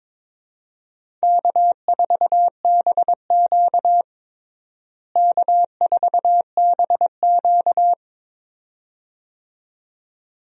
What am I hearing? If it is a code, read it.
Morse code, heard as a keyed tone, about 21 wpm, 700 Hz. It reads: K4BQ K4BQ
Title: USA Call Signs - 4 Characters - Repeated 2x - 22wpm